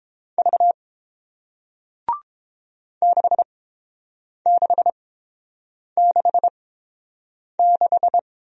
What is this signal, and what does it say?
Morse code, heard as a keyed tone, about 30 wpm, spaced as for 7 wpm, 700 Hz. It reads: V 6666